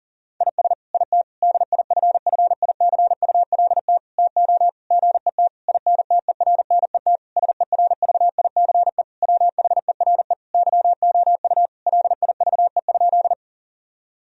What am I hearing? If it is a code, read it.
Morse code, heard as a keyed tone, about 40 wpm, 700 Hz. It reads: IS IT DIFFICULT TO GET INTERNET SERVICE WHERE YOU LIVE?
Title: Sentences from Top 1000 Words - Review Part 4 - 40wpm